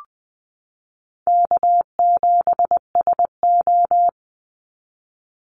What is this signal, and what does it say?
Morse code, heard as a keyed tone, about 20 wpm, 700 Hz. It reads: K7SO
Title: NA SOTA Activator Callsigns 20wpm